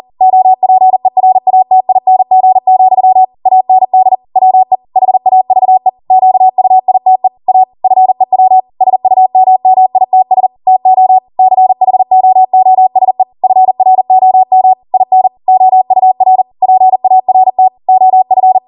Morse code, heard as a keyed tone, about 40 wpm, 750 Hz. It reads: OPERATING, AND WE HAVE QUITE A FEW SUMMITS TO CHOOSE FROM IN OUR PART OF